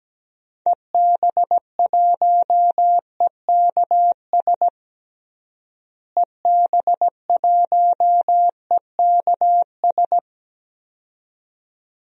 Morse code, heard as a keyed tone, about 17 wpm, 700 Hz. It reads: EB1EKS EB1EKS
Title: DXCC-100 Call Signs - 6 Characters - Repeated 2x - 17wpm